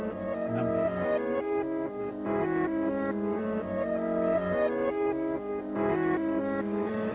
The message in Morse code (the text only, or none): none